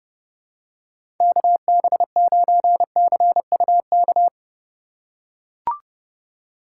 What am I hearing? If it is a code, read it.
Morse code, heard as a keyed tone, about 30 wpm, 700 Hz. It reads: KB9CUK
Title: USA Call Signs - 6 Characters - 2x Character Spacing - 30wpm